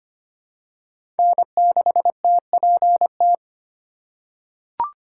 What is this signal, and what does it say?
Morse code, heard as a keyed tone, about 25 wpm, 700 Hz. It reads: N6TPT
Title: USA Call Signs - 5 Characters - 4x Character Spacing - 25wpm